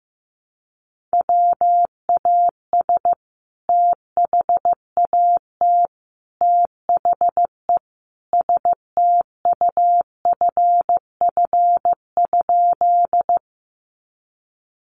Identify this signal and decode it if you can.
Morse code, heard as a keyed tone, about 15 wpm, 700 Hz. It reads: WAS THAT THE STUFF?
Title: Sentences from Top 1400 Words 15wpm